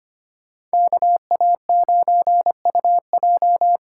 Morse code, heard as a keyed tone, about 25 wpm, 700 Hz. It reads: KA9UJ
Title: USA Call Signs - 6 Characters - 8x Character Spacing - 25wpm